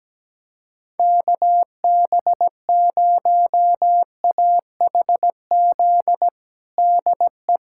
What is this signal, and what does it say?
Morse code, heard as a keyed tone, about 17 wpm, 700 Hz. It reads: KB0AHZ DE
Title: USA Callsign DE Callsign - 6 Characters - 17wpm